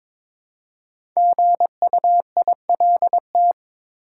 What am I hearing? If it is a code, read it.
Morse code, heard as a keyed tone, about 22 wpm, 700 Hz. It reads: GUILT
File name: 5 Letter Words - First 3 Letters Given - 22wpm